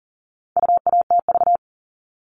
Morse code, heard as a keyed tone, about 40 wpm, 700 Hz. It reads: UATV
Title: All Cut Numbers - 4-Digit 40wpm